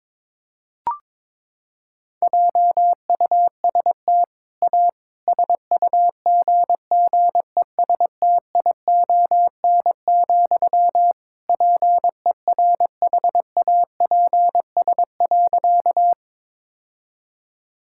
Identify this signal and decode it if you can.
Morse code, heard as a keyed tone, about 22 wpm, 700 Hz. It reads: JUST A SUGGESTION, PERHAPS.
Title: Sentences from Top 1600 Words 22wpm